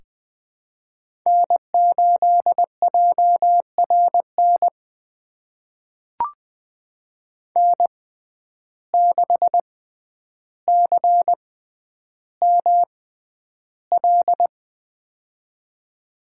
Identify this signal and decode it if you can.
Morse code, heard as a keyed tone, about 20 wpm, spaced as for 10 wpm, 700 Hz. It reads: N8JRN N6CML
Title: USA Call Signs - 5 Characters - 6x Character Spacing - 20wpm